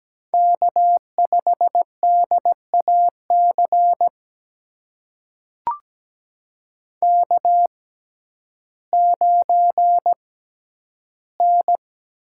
Morse code, heard as a keyed tone, about 17 wpm, spaced as for 10 wpm, 700 Hz. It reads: K5DAC K9N
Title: USA Call Signs - 5 Characters - 6x Character Spacing - 17wpm